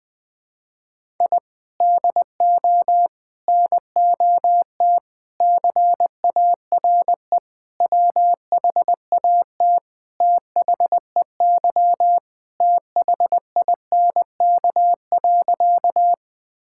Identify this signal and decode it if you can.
Morse code, heard as a keyed tone, about 20 wpm, 700 Hz. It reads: I DO NOT CARE WHAT THEY THINK.
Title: Sentences with 5-Word Phrases - 20wpm